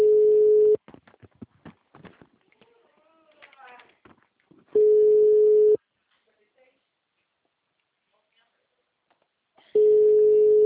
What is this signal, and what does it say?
Ringback tone (Europe, 425 Hz), with speech.